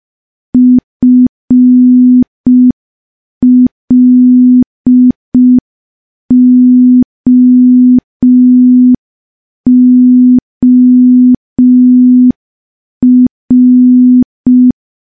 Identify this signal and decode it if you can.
Morse code, heard as a keyed tone, about 5 wpm, 250 Hz. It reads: FLOOR
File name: morse / morse-training-05WPM